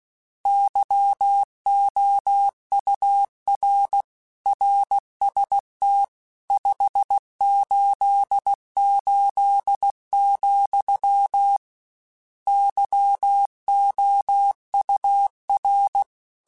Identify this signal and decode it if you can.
Morse code, heard as a keyed tone, about 16 wpm, 800 Hz. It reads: YOUR RST 588, YOUR